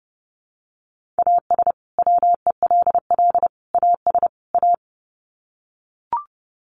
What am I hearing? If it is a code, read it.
Morse code, heard as a keyed tone, about 30 wpm, 700 Hz. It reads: AS WELL AS A
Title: Sets of 4 Words 30wpm